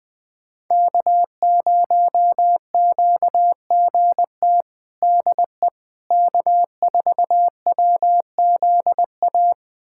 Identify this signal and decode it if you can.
Morse code, heard as a keyed tone, about 20 wpm, 700 Hz. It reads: K0QGT DE K4WZA
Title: USA Callsign DE Callsign - 5 Characters - 20wpm